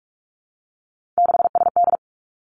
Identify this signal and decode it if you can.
Morse code, heard as a keyed tone, about 46 wpm, 700 Hz. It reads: 6SD
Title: Three Challenging Letter-Number Combinations 45wpm